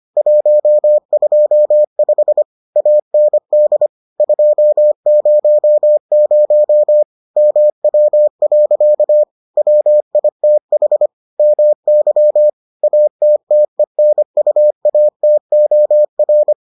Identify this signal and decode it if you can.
Morse code, heard as a keyed tone, about 25 wpm, 600 Hz. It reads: 125 AND 200 MW. WITH MY ATTENUATOR